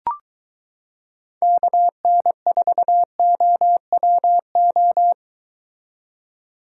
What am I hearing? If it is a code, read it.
Morse code, heard as a keyed tone, about 22 wpm, 700 Hz. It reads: KN4OWO